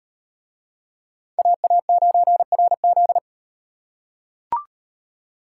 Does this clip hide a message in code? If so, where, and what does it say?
Yes: Morse code from 1.4 to 3.2 s, heard as a keyed tone, about 38 wpm, 700 Hz. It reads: AA9RZ